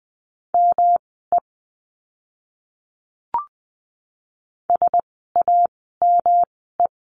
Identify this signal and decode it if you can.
Morse code, heard as a keyed tone, about 20 wpm, spaced as for 14 wpm, 700 Hz. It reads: ME SAME
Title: 115 Lesson 19f - Letter w - Words 20wpm